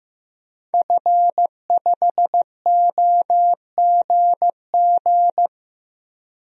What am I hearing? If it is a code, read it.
Morse code, heard as a keyed tone, about 16 wpm, 700 Hz. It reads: F5OGG
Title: DXCC-100 Call Signs - 5 Characters - 15wpm